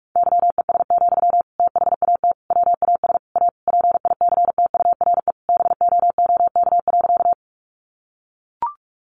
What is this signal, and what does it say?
Morse code, heard as a keyed tone, about 44 wpm, 700 Hz. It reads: YES, THAT WAS A PICTURE BOOK.
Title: Sentences from Top 500 Words 45wpm